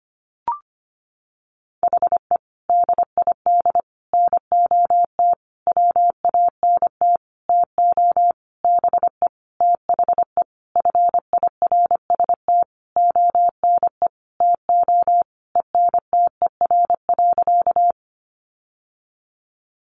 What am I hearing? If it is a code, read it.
Morse code, heard as a keyed tone, about 25 wpm, 700 Hz. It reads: HE DID NOT WANT TO BE THE FIRST ONE TO ENTER.
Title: Sentences from Top 1300 Words 25wpm